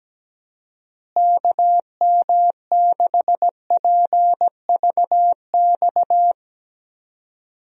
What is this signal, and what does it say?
Morse code, heard as a keyed tone, about 17 wpm, 700 Hz. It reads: KM6PVX
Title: USA Call Signs - 6 Characters - Repeated 3x - 17wpm